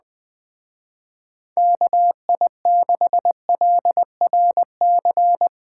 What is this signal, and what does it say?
Morse code, heard as a keyed tone, about 20 wpm, 700 Hz. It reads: KI6LRC